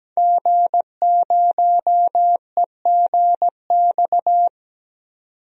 Morse code, heard as a keyed tone, about 17 wpm, 700 Hz. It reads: G0EGX
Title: DXCC-100 Call Signs - 5 Characters - 17wpm